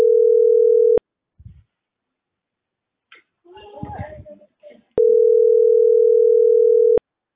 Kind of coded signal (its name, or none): telephone call progress tone